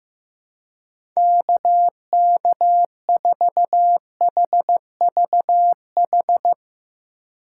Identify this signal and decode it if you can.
Morse code, heard as a keyed tone, about 15 wpm, 700 Hz. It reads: KK4HVH
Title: USA Call Signs - 4x Character Spacing - 15wpm